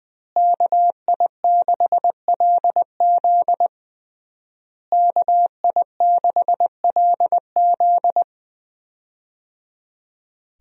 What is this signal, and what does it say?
Morse code, heard as a keyed tone, about 20 wpm, 700 Hz. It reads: KI6LZ KI6LZ
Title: USA Call Signs - 5 Characters - Repeated 3x - 20wpm